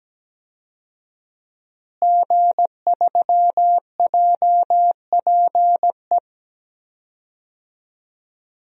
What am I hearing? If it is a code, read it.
Morse code, heard as a keyed tone, about 17 wpm, 700 Hz. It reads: G3JPE
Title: DXCC-100 Call Signs - 5 Characters - Repeated 3x - 17wpm